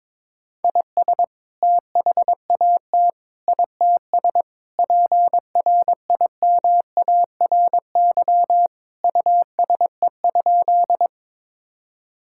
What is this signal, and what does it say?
Morse code, heard as a keyed tone, about 22 wpm, 700 Hz. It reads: IS THAT ITS PRIMARY USE?
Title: Sentences from Top 1000 Words - Review Part 5 - 22wpm